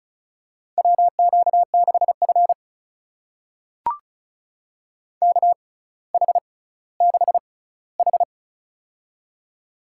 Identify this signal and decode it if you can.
Morse code, heard as a keyed tone, about 36 wpm, spaced as for 18 wpm, 700 Hz. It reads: WQ6F KH6H
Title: USA Call Signs - 4 Characters - 6x Character Spacing - 35wpm